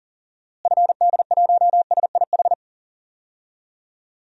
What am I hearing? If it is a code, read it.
Morse code, heard as a keyed tone, about 40 wpm, 700 Hz. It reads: FD1SIH
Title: DXCC-100 Call Signs - 6 Characters - Repeated 3x - 40wpm